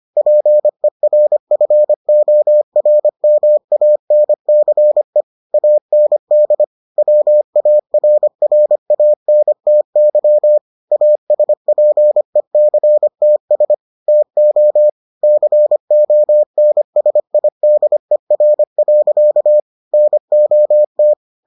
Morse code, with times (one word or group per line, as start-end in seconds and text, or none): 0.2-5.2 PERFORMANCE
5.5-6.6 AND
7.0-10.6 WARRANTY
10.9-13.7 ASPECTS
14.1-14.9 TO
15.2-19.6 CONSIDER.
19.9-21.1 NOT